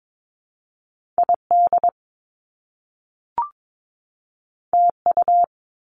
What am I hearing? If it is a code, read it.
Morse code, heard as a keyed tone, about 22 wpm, 700 Hz. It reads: ID TU
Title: Top Two Letter Combinations in Words 22wpm